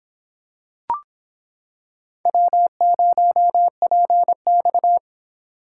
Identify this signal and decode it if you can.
Morse code, heard as a keyed tone, about 25 wpm, 700 Hz. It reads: W0PX